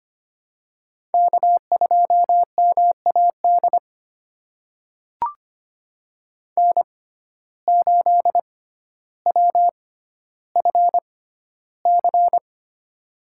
Morse code, heard as a keyed tone, about 25 wpm, spaced as for 13 wpm, 700 Hz. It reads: K2MAD N8WFC
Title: USA Call Signs - 5 Characters - 6x Character Spacing - 25wpm